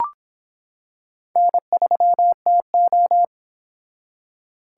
Morse code, heard as a keyed tone, about 25 wpm, 700 Hz. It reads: N3TO